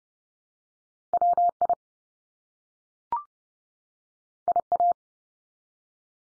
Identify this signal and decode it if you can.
Morse code, heard as a keyed tone, about 30 wpm, 700 Hz. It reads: WI IA
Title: US States 30WPM